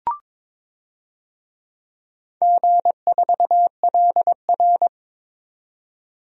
Morse code, heard as a keyed tone, about 22 wpm, 700 Hz. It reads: G4LR
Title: DXCC-100 Call Signs - 4 Characters - 22wpm